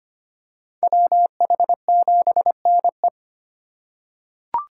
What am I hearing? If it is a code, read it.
Morse code, heard as a keyed tone, about 25 wpm, 700 Hz. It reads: WH7NE
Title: USA Call Signs - 5 Characters - 3x Character Spacing - 25wpm